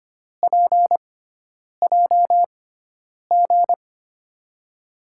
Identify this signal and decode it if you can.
Morse code, heard as a keyed tone, about 25 wpm, spaced as for 9 wpm, 700 Hz. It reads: PJG
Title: USA Call Signs - 5 Characters - 6x Character Spacing - 25wpm